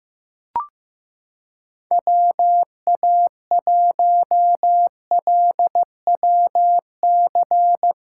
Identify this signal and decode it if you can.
Morse code, heard as a keyed tone, about 15 wpm, 700 Hz. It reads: WA1LWC
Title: K1USN SST QSO Practice - 15wpm